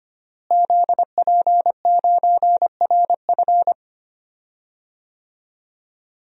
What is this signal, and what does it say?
Morse code, heard as a keyed tone, about 25 wpm, 700 Hz. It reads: ZP9RF
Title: DXCC-100 Call Signs - 5 Characters - Repeated 3x - 25wpm